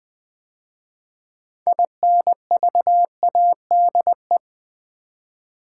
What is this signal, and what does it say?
Morse code, heard as a keyed tone, about 20 wpm, 700 Hz. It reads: INVADE
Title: 6 Letter Words - First 3 Letters Given - 20wpm